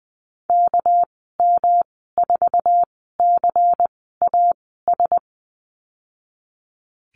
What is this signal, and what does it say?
Morse code, heard as a keyed tone, about 20 wpm, spaced as for 14 wpm, 700 Hz. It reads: KM4CAS
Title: 245 Lesson 39g - Letter z - Call Signs 20wpm